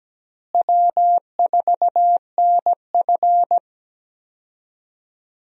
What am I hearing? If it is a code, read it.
Morse code, heard as a keyed tone, about 17 wpm, 700 Hz. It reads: W4NF
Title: USA Call Signs - 4 Characters - Repeated 2x - 17wpm